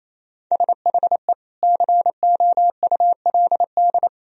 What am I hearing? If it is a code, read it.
Morse code, heard as a keyed tone, about 28 wpm, 700 Hz. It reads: SHE COULD
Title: Sentences from Top 1000 Words - Review Part 3 - 28wpm